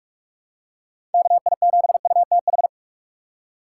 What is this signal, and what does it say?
Morse code, heard as a keyed tone, about 46 wpm, 700 Hz. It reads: KI7UTH
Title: USA Call Signs - 6 Characters - 4x Character Spacing - 45wpm